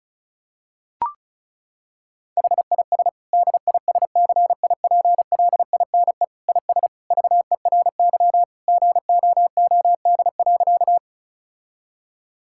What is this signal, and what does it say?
Morse code, heard as a keyed tone, about 36 wpm, 700 Hz. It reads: HIS DISCIPLINE IS VERY GOOD.
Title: Sentences from Top 1000 Words - Review Part 6 - 35wpm